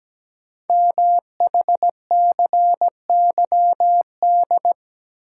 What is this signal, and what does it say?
Morse code, heard as a keyed tone, about 17 wpm, 700 Hz. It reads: MHCYD
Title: Code Groups - 5 Letters - 17wpm